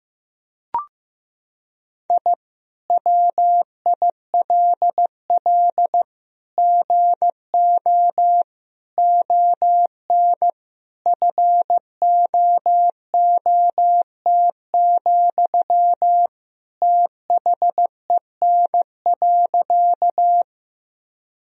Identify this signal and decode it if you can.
Morse code, heard as a keyed tone, about 15 wpm, 700 Hz. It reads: I WILL GO ON FOOT, THEN.
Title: Sentences from Top 1000 Words - Review Part 3 - 15wpm